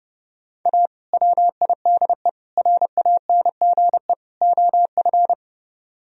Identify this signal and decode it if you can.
Morse code, heard as a keyed tone, about 30 wpm, 700 Hz. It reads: A WIDE RANGE OF